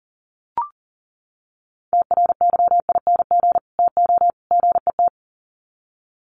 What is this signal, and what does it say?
Morse code, heard as a keyed tone, about 40 wpm, 700 Hz. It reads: TRYING TO GET
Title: Sets of 3 Words 40wpm